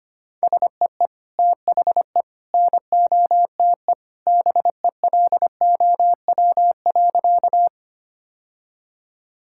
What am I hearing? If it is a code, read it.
Morse code, heard as a keyed tone, about 25 wpm, 700 Hz. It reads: SEE THE NOTE BELOW.
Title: Sentences from Top 1000 Words - Review Part 5 - 25wpm